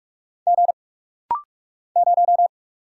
Morse code, heard as a keyed tone, about 44 wpm, 700 Hz. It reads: G 0
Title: Single Letter-Number - Rapid-Fire 45wpm